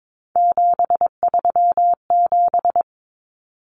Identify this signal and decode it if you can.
Morse code, heard as a keyed tone, about 22 wpm, 700 Hz. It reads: 737
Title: Common Cut Numbers - 3-Digit 22wpm